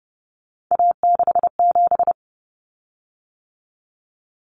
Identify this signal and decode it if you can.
Morse code, heard as a keyed tone, about 30 wpm, 700 Hz. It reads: A67
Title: Common Cut Numbers - 3-Digit 30wpm